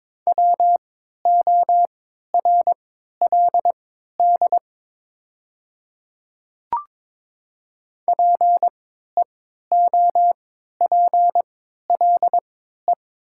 Morse code, heard as a keyed tone, about 22 wpm, spaced as for 12 wpm, 700 Hz. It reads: WORLD PEOPLE
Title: 3x Character Spacing - Top 200 Words - 22wpm